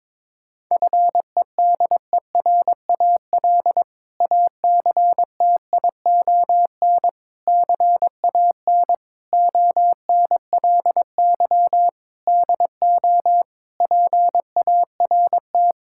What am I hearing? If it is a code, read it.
Morse code, heard as a keyed tone, about 21 wpm, 700 Hz. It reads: FEDERAL ACTION CAN ONLY DO PART
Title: Sentences from Top 1000 Words - Review Part 5 - 22wpm